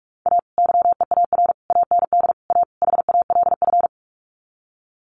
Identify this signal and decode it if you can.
Morse code, heard as a keyed tone, about 44 wpm, 700 Hz. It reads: A YEAR AND A HALF
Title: Sets of 5 Words 45wpm